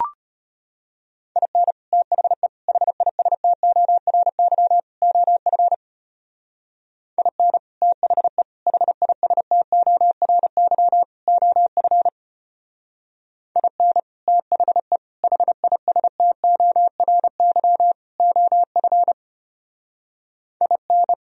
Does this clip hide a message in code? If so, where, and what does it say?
Yes: Morse code from 1.4 to 21.1 s, heard as a keyed tone, about 32 wpm, 690 Hz. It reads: IN THE HISTORY OF IN THE HISTORY OF IN THE HISTORY OF IN